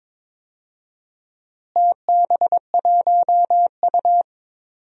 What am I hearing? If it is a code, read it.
Morse code, heard as a keyed tone, about 22 wpm, 700 Hz. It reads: TB1U
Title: DXCC-100 Call Signs - 4 Characters - Repeated 2x - 22wpm